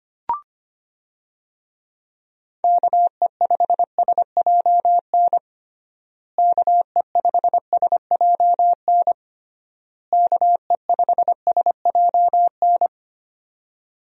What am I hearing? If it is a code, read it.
Morse code, heard as a keyed tone, about 25 wpm, 700 Hz. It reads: KE5SJN KE5SJN KE5SJN
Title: USA Call Signs - 6 Characters - Repeated 3x - 25wpm